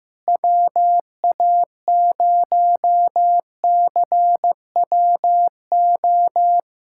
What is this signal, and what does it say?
Morse code, heard as a keyed tone, about 15 wpm, 700 Hz. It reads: WA0CWO
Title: USA Callsign DE Callsign - 6 Characters - 15wpm